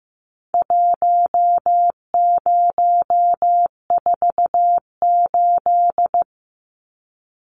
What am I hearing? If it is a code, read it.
Morse code, heard as a keyed tone, about 15 wpm, 700 Hz. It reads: 1048
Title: Numbers - Contest Style - 4-Digit 15wpm